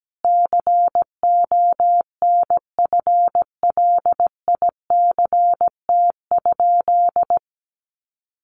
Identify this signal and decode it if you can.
Morse code, heard as a keyed tone, about 17 wpm, 700 Hz. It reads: CONFLICT?
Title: Sentences from Top 1300 Words 17wpm